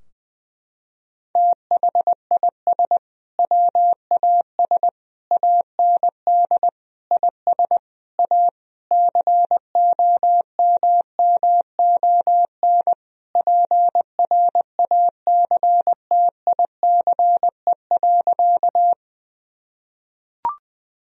Morse code, heard as a keyed tone, about 21 wpm, 700 Hz. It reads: THIS WAS AND IS A COMMON PRACTICE.
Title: Sentences from Top 1000 Words - Review Part 4 - 20wpm